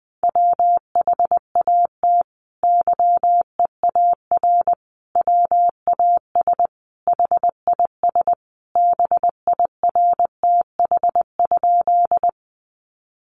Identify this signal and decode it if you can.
Morse code, heard as a keyed tone, about 20 wpm, 700 Hz. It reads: WHAT YEAR WAS HIS BIRTH?
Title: Sentences from Top 800 Words 20wpm